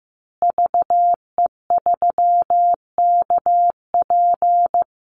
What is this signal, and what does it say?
Morse code, heard as a keyed tone, about 15 wpm, 700 Hz. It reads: VE3KP
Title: Call Signs Repeated 3x - 15wpm